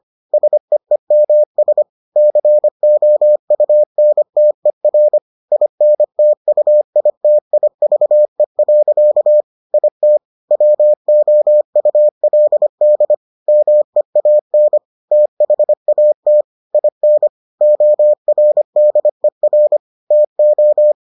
Morse code, heard as a keyed tone, about 25 wpm, 600 Hz. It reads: SEEMS COUNTER INTUITIVE. IT WOULD MEAN THAT IN ORDER TO